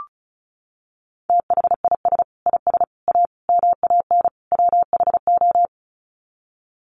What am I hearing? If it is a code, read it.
Morse code, heard as a keyed tone, about 34 wpm, 700 Hz. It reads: THIS IS A MAN WHO
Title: Sets of 5 Words Encore2 35wpm